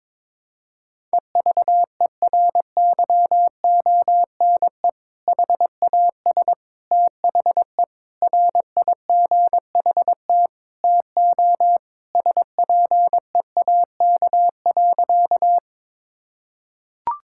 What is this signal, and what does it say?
Morse code, heard as a keyed tone, about 22 wpm, 700 Hz. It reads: EVERYONE HAS THE RIGHT TO SPEAK.
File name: practice / 22wpm / Sentences with 3-Word Phrases - 22wpm